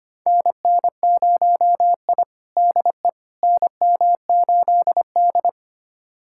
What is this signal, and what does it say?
Morse code, heard as a keyed tone, about 25 wpm, 700 Hz. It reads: NN0I DE NM8D
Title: USA Callsign DE Callsign - 4 Characters - 25wpm